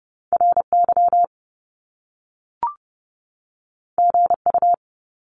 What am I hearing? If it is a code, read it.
Morse code, heard as a keyed tone, about 30 wpm, 700 Hz. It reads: RY GU
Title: Top Two Letter Combinations in Words 30wpm